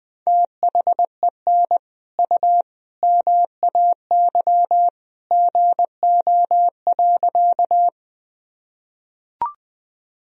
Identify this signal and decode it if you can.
Morse code, heard as a keyed tone, about 20 wpm, 700 Hz. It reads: THEN U MAY GO.
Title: Sentences from Top 100 Words and Abbreviations 20wpm